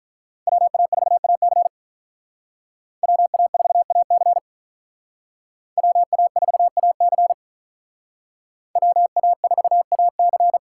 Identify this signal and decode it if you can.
Morse code, heard as a keyed tone, about 45 wpm, 700 Hz. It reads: WA4AC WA4AC WA4AC WA4AC